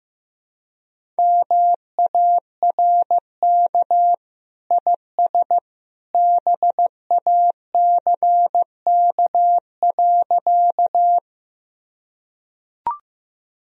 Morse code, heard as a keyed tone, about 15 wpm, 700 Hz. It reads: MARK IS BACK.
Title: Sentences from Top 1000 Words - Review Part 1 - 15wpm